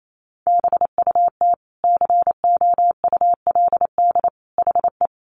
Morse code, heard as a keyed tone, about 28 wpm, 700 Hz. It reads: BUT COULD HE
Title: Sentences from Top 1200 Words 28wpm